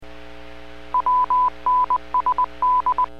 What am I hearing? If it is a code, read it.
Morse code, heard as a keyed tone, about 20 wpm, 1000 Hz. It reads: WNSD